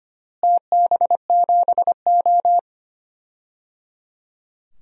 Morse code, heard as a keyed tone, about 25 wpm, 700 Hz. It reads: TB7O
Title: DXCC-100 Call Signs - 4 Characters - Repeated 3x - 25wpm